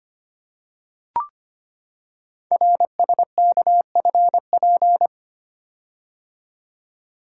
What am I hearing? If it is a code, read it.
Morse code, heard as a keyed tone, about 25 wpm, 700 Hz. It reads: RSKFP
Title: Code Groups - 5 Letters - 25wpm